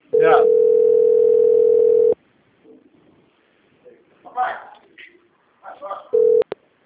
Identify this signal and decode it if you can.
Ringback tone (North America), 440 + 480 Hz, with speech.
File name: out- - - - - .6959